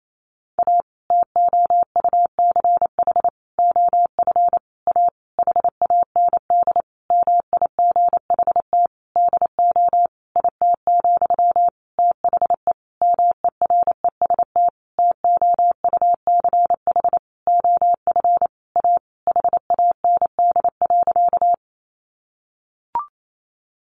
Morse code, heard as a keyed tone, about 28 wpm, 700 Hz. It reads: A TOUCH OF A HAND MIGHT DO IT, THE MEREST TOUCH OF A HAND.
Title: A Sound Of Thunder 28wpm